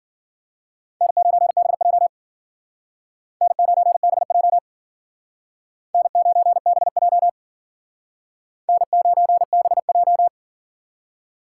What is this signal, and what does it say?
Morse code, heard as a keyed tone, about 48 wpm, 700 Hz. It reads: N9BJ N9BJ N9BJ N9BJ
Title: Speed Racing - USA 1x2 Call Signs 40wpm